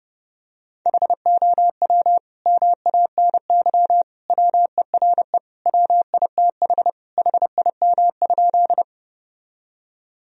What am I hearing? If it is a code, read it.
Morse code, heard as a keyed tone, about 30 wpm, 700 Hz. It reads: HOW MANY WERE WITH HIM?
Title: Sentences from Top 1000 Words - Review Part 1 - 30wpm